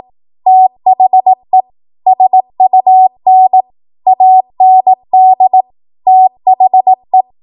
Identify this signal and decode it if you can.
Morse code, heard as a keyed tone, about 18 wpm, 750 Hz. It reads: THE SUN AND THE
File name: Morse / 18WPM